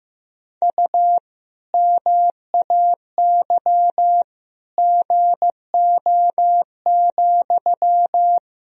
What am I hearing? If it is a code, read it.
Morse code, heard as a keyed tone, about 15 wpm, 700 Hz. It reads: U MAY GO,
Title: Sentences from Top 400 Words and Abbreviations 15wpm